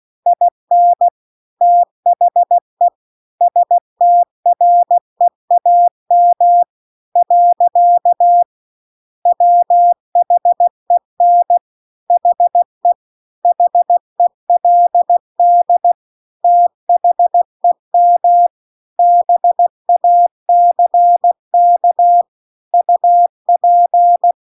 Morse code, heard as a keyed tone, about 16 wpm, 700 Hz. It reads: IN THE STREAM . WHEN HE HELD THEM BACK UP
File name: CW-The-Big-Twoheaded-River-2-16WPM